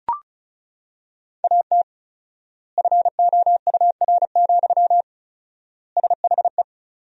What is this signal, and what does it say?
Morse code, heard as a keyed tone, about 36 wpm, 700 Hz. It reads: AT FOUR, SHE
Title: 4x Word-Spacing - Sentences from Top 300 Words - 35wpm